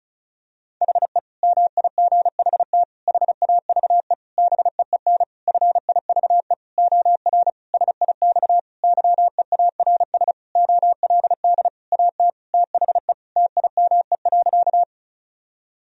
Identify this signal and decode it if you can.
Morse code, heard as a keyed tone, about 36 wpm, 700 Hz. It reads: HE MIGHT HAVE BEEN FIVE OR SIX YEARS OLD AT THE TIME.
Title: Sentences from Top 1000 Words - Review Part 2 - 35wpm